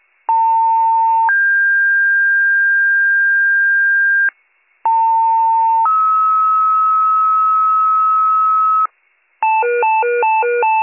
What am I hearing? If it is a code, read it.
Two-tone page, heard as a one-second tone, then a three-second tone, about 900 then 1600 Hz, about 900 then 1290 Hz, with noise. It is heard from 0.3 to 8.9 s.